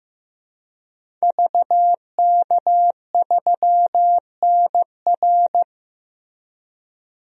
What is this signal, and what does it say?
Morse code, heard as a keyed tone, about 15 wpm, 700 Hz. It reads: VK3NR
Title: DXCC-100 Call Signs - 5 Characters - Repeated 3x - 15wpm